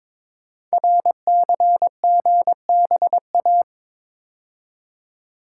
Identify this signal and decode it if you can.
Morse code, heard as a keyed tone, about 22 wpm, 700 Hz. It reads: RCGBA
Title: Code Groups - 5 Letters - 22wpm